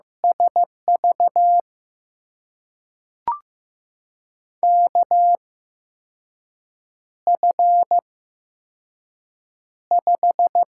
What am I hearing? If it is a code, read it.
Morse code, heard as a keyed tone, about 15 wpm, spaced as for 5 wpm, 700 Hz. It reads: SV KF5
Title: USA Call Signs - 6 Characters - 8x Character Spacing - 15wpm